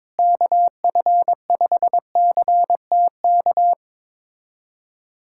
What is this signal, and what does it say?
Morse code, heard as a keyed tone, about 22 wpm, 700 Hz. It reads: KF5CTK
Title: USA Callsign DE Callsign 22wpm